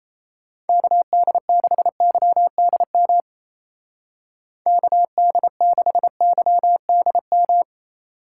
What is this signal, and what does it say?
Morse code, heard as a keyed tone, about 30 wpm, 700 Hz. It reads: KD6YDM KD6YDM
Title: Speed Racing - USA Call Signs 28wpm